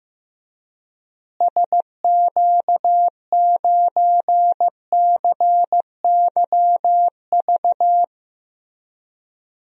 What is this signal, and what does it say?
Morse code, heard as a keyed tone, about 15 wpm, 700 Hz. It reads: SQ9CYV
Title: DXCC-100 Call Signs - 6 Characters - Repeated 3x - 15wpm